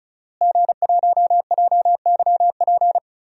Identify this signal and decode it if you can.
Morse code, heard as a keyed tone, about 34 wpm, 700 Hz. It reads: G1JYP
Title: DXCC-100 Call Signs - 5 Characters - Repeated 3x - 35wpm